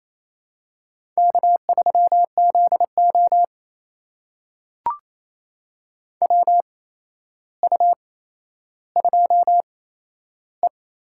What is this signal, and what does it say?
Morse code, heard as a keyed tone, about 28 wpm, spaced as for 12 wpm, 700 Hz. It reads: K3ZO WU2E
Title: USA Call Signs - 4 Characters - 8x Character Spacing - 28wpm